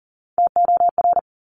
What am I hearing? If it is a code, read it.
Morse code, heard as a keyed tone, about 38 wpm, 700 Hz. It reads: TOR
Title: Top Three Letter Combinations in Words 40wpm